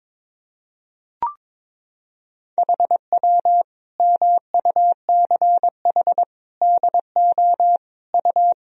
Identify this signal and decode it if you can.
Morse code, heard as a keyed tone, about 22 wpm, 700 Hz. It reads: HW MUCH DO U
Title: Sentences from Top 200 Words and Abbreviations 22wpm